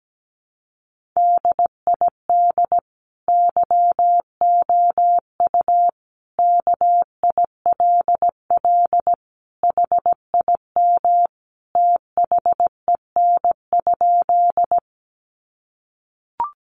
Morse code, heard as a keyed tone, about 17 wpm, 700 Hz. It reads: DID YOU KILL HIM THEN?
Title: Sentences from Top 1400 Words 17wpm